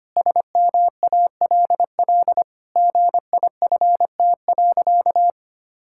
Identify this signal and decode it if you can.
Morse code, heard as a keyed tone, about 25 wpm, 700 Hz. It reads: SMALL GIFT.
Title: Sentences from Top 1000 Words - Review Part 6 - 25wpm